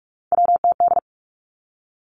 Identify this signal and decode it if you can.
Morse code, heard as a keyed tone, about 46 wpm, 700 Hz. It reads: WTD
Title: Three Challenging Letter-Number Combinations 45wpm